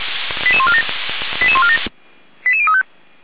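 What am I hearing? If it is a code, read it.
Five-tone selective call (ZVEI), heard as quick runs of five tones, tones of about 70 ms. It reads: 88147 88147 88035
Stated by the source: ZVEI five-tone call 88501 received by SDR, recorded by feuerwehr-orlamuende (f85360) fire brigade alerting server.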